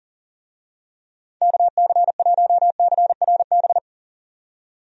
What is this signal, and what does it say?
Morse code, heard as a keyed tone, about 40 wpm, 700 Hz. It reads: KC1CRB
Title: USA Call Signs - 6 Characters - 40wpm